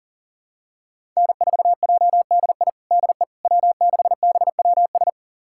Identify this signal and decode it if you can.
Morse code, heard as a keyed tone, about 40 wpm, 700 Hz. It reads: N4JDI DE W6BWS
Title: USA Callsign DE Callsign - 5 Characters - 40wpm